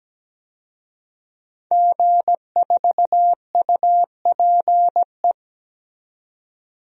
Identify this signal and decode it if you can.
Morse code, heard as a keyed tone, about 17 wpm, 700 Hz. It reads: G4UPE
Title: DXCC-100 Call Signs - 5 Characters - 17wpm